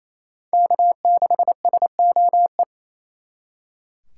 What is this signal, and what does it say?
Morse code, heard as a keyed tone, about 35 wpm, 700 Hz. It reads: K6SOE
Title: Speed Racing - USA 1x3 Call Signs 28wpm